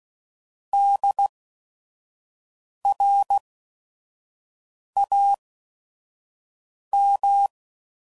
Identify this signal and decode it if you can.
Morse code, heard as a keyed tone, about 16 wpm, spaced as for 5 wpm, 800 Hz. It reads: DRAM
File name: Prefixes / By First Letter / Ds